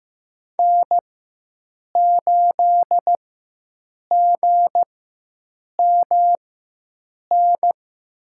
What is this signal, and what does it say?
Morse code, heard as a keyed tone, about 15 wpm, spaced as for 7 wpm, 700 Hz. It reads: N8GMN
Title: USA Call Signs - 5 Characters - 4x Character Spacing - 15wpm